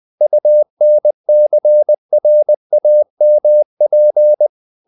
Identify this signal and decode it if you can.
Morse code, heard as a keyed tone, about 20 wpm, 600 Hz. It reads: UNCRAMP
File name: the old man and the sea / the old man and the sea part 1 20wpm